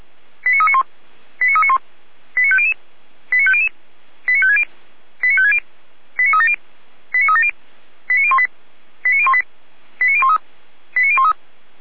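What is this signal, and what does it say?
Five-tone selective call (ZVEI), heard as quick runs of five tones, tones of about 70 ms. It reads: 89391 89391 89500 89500 89560 89560 89370 89370 89018 89018 89013 89013
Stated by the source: ZVEI five-tone selective call received on an SDR.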